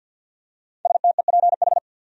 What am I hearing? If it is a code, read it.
Morse code, heard as a keyed tone, about 50 wpm, 700 Hz. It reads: STEPH
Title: USA Common Names - 50wpm